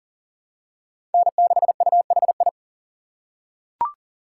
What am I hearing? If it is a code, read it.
Morse code, heard as a keyed tone, about 39 wpm, 700 Hz. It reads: N6UHI
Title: USA Call Signs - 5 Characters - 2x Character Spacing - 40wpm